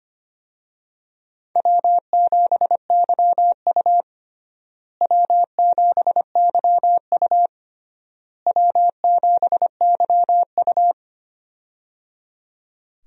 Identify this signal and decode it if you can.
Morse code, heard as a keyed tone, about 25 wpm, 700 Hz. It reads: W7YU W7YU W7YU
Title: USA Call Signs - 4 Characters - Repeated 3x - 25wpm